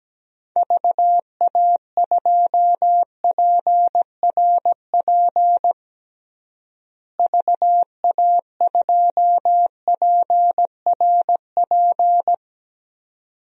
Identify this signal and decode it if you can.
Morse code, heard as a keyed tone, about 17 wpm, 700 Hz. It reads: VA2PRP VA2PRP
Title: DXCC-100 Call Signs - 6 Characters - Repeated 2x - 17wpm